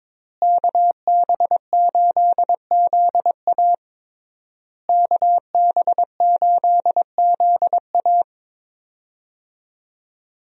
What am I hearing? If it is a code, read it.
Morse code, heard as a keyed tone, about 22 wpm, 700 Hz. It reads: KB8ZA KB8ZA
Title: USA Call Signs - 5 Characters - Repeated 3x - 22wpm